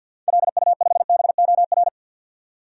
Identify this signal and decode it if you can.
Morse code, heard as a keyed tone, about 53 wpm, 700 Hz. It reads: LU5BOR
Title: DXCC-100 Call Signs - 6 Characters - 50wpm